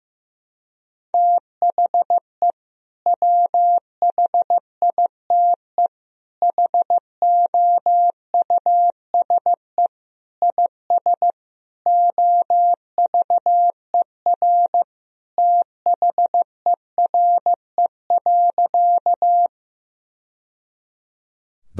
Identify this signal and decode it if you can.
Morse code, heard as a keyed tone, about 15 wpm, 700 Hz. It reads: THE WHITE HOUSE IS OVER THERE.
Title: Sentences from Top 1000 Words - Review Part 1 - 15wpm